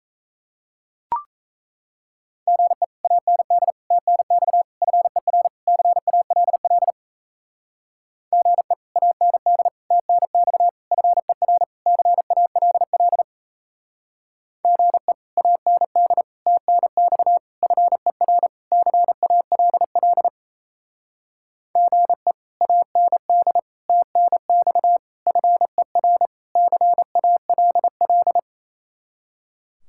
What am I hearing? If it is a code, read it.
Morse code, heard as a keyed tone, about 33 wpm, 700 Hz. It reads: GE AND TNX FER CALL GE AND TNX FER CALL GE AND TNX FER CALL GE AND TNX FER CALL
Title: Speed-Racing Common QSO Phrases - 28wpm